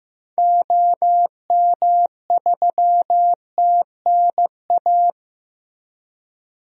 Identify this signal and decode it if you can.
Morse code, heard as a keyed tone, about 15 wpm, 700 Hz. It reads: OM3TNA
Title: DXCC-100 Call Signs - 6 Characters - Repeated 3x - 15wpm